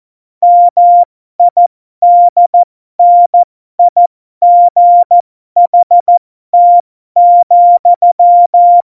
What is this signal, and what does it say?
Morse code, heard as a keyed tone, about 14 wpm, spaced as for 12 wpm, 700 Hz. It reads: MIDNIGHT,
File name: Morsekurs / Witze